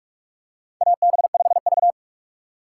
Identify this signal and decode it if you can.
Morse code, heard as a keyed tone, about 46 wpm, 700 Hz. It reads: AB5V
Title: USA Call Signs - 4 Characters - 3x Character Spacing - 45wpm